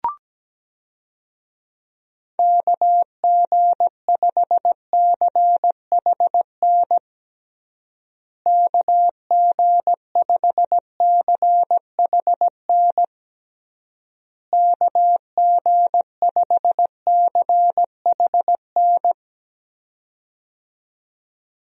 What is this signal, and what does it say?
Morse code, heard as a keyed tone, about 17 wpm, 700 Hz. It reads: KG5CHN KG5CHN KG5CHN
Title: USA Call Signs - 6 Characters - Repeated 3x - 17wpm